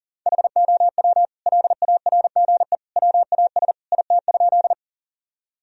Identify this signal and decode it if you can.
Morse code, heard as a keyed tone, about 40 wpm, 700 Hz. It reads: HOW LARGE WAS IT?
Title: Sentences from Top 1000 Words - Review Part 1 - 40wpm